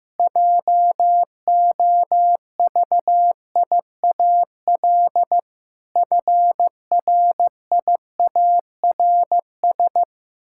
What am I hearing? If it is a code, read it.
Morse code, heard as a keyed tone, about 15 wpm, 700 Hz. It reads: JOVIAL FRIARS
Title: Pangram Challenge v2 - 15wpm